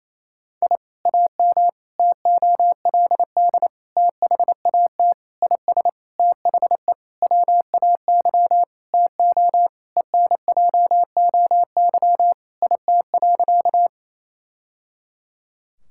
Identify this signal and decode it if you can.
Morse code, heard as a keyed tone, about 28 wpm, 700 Hz. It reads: I AM TOLD THAT IS THE WAY TO ENJOY IT.
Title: Sentences from Top 1000 Words - Review Part 5 - 28wpm